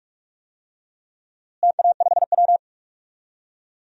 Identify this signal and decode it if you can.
Morse code, heard as a keyed tone, about 46 wpm, 700 Hz. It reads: TA5W
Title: DXCC-100 Call Signs - 4 Characters - Repeated 2x - 45wpm